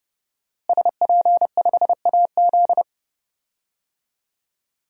Morse code, heard as a keyed tone, about 29 wpm, 700 Hz. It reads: SP5AZ
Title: DXCC-100 Call Signs - 5 Characters - Repeated 3x - 30wpm